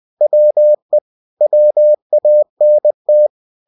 Morse code, heard as a keyed tone, about 20 wpm, 600 Hz. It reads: WE WANT
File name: cw-books / winnie-the-pooh / winnie-20wpm-page-0078